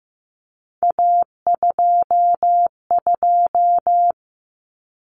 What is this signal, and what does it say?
Morse code, heard as a keyed tone, about 15 wpm, 700 Hz. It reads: A22
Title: Common Cut Numbers - 3-Digit 15wpm